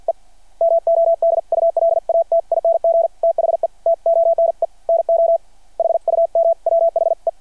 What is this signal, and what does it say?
Morse code, heard as a keyed tone, about 53 wpm, 690 Hz. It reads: E MODULATING THE TONE NO HUM WHE
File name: archived / CRSnet